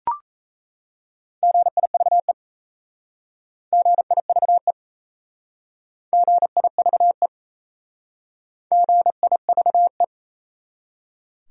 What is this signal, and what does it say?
Morse code, heard as a keyed tone, about 36 wpm, 700 Hz. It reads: GIVE GIVE GIVE GIVE